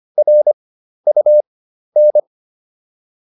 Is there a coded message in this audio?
Yes: Morse code from 0.2 to 2.2 s, heard as a keyed tone, about 25 wpm, spaced as for 12 wpm, 600 Hz. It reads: RUN